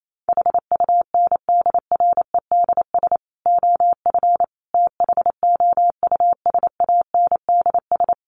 Morse code, heard as a keyed tone, about 28 wpm, 700 Hz. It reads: HUNDREDS OF THOUSANDS
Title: Sets of 5 Words Encore2 28wpm